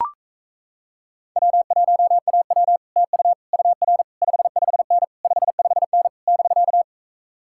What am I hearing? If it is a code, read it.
Morse code, heard as a keyed tone, about 37 wpm, 700 Hz. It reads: W1AW TU UR 55N 55N <BK>